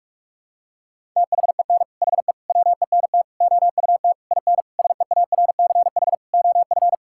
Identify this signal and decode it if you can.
Morse code, heard as a keyed tone, about 46 wpm, 700 Hz. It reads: THEN HE WENT OUT IN SEARCH OF
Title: Sentences from Top 1000 Words - Review Part 5 - 45wpm